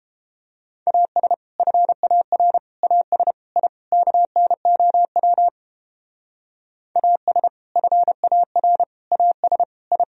Morse code, heard as a keyed tone, about 31 wpm, 700 Hz. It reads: AS FAR AS I KNOW AS FAR AS I